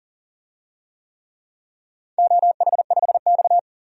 Morse code, heard as a keyed tone, about 40 wpm, 690 Hz. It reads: OH5X
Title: DXCC-100 Call Signs - 4 Characters - 40wpm